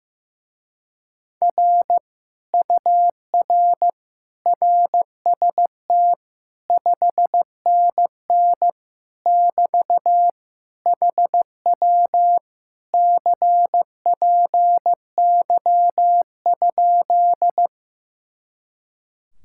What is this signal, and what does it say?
Morse code, heard as a keyed tone, about 15 wpm, 700 Hz. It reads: R UR RST 5NN <BT> HW CPY?
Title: Speed-Racing Common QSO Phrases Encore - 15wpm